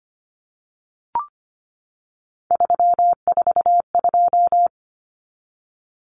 Morse code, heard as a keyed tone, about 25 wpm, 700 Hz. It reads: 342
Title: Numbers - 3-Digit 25wpm